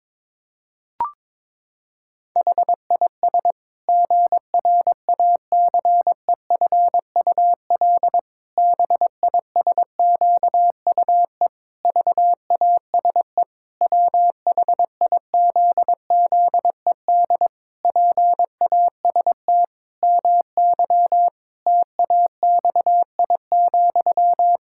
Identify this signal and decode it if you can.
Morse code, heard as a keyed tone, about 22 wpm, 700 Hz. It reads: HIS GRACEFUL BISQUE VASE WHIZZED PAST MY TAXI,
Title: Pangram Challenge v2 - 22wpm